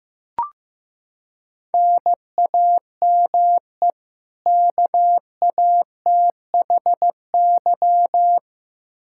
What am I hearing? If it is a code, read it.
Morse code, heard as a keyed tone, about 16 wpm, 700 Hz. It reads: NAME KATHY
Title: Practice QSO - Exchange 1 - Beginning - 15wpm